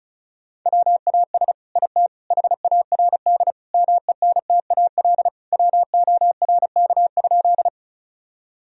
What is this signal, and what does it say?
Morse code, heard as a keyed tone, about 36 wpm, 700 Hz. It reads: WAS IT HARD MENTAL WORK?
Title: Sentences from Top 1000 Words - Review Part 4 - 35wpm